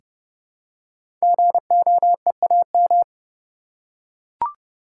Morse code, heard as a keyed tone, about 29 wpm, 700 Hz. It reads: GOEAM
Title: Code Groups - 5 Letters - 30wpm